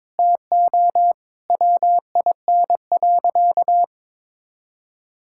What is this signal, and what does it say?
Morse code, heard as a keyed tone, about 22 wpm, 700 Hz. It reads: TO WIN.
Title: Sentences from Top 1600 Words 22wpm